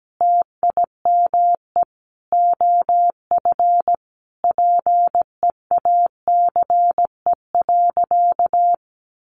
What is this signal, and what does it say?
Morse code, heard as a keyed tone, about 17 wpm, 700 Hz. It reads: TIME OF PEACE.
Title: Sentences from Top 1500 Words 17wpm